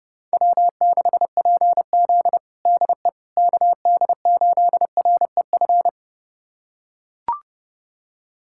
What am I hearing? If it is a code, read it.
Morse code, heard as a keyed tone, about 30 wpm, 700 Hz. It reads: W6PZ DE KD8REF
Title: USA Callsign DE Callsign 30wpm